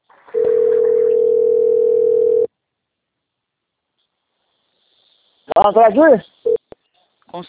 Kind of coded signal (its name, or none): telephone call progress tone